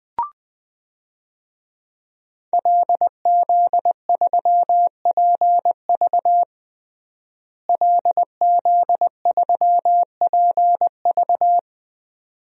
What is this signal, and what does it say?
Morse code, heard as a keyed tone, about 21 wpm, 700 Hz. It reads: LZ3PV LZ3PV
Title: DXCC-100 Call Signs - 5 Characters - Repeated 2x - 20wpm